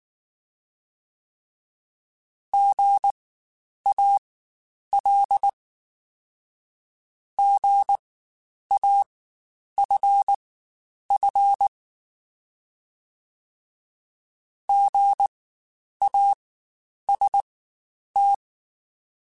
Morse code, heard as a keyed tone, about 19 wpm, spaced as for 9 wpm, 800 Hz. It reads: GAL GAFF GAST
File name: Spaced Alphas / Spaced Alphas mp3s / Gs sp @18